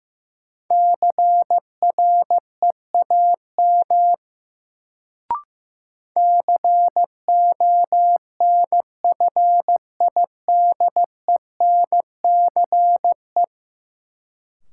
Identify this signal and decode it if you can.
Morse code, heard as a keyed tone, about 15 wpm, 700 Hz. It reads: CREAM CONFIDENCE
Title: Sentences from Top 1500 Words - New Word Drill Alt - 15wpm